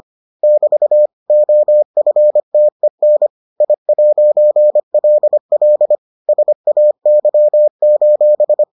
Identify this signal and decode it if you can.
Morse code, heard as a keyed tone, about 25 wpm, 600 Hz. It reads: <BT> OFTEN I'LL SAY: